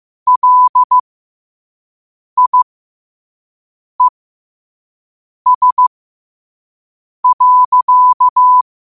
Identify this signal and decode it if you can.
Morse code, heard as a keyed tone, about 15 wpm, spaced as for 5 wpm, 1000 Hz. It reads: LIES.